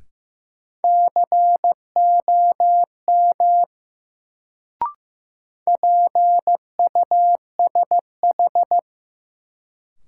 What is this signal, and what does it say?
Morse code, heard as a keyed tone, about 15 wpm, 700 Hz. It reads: COM PUSH
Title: Sentences from Top 1300 Words - New Word Drill Alt - 15wpm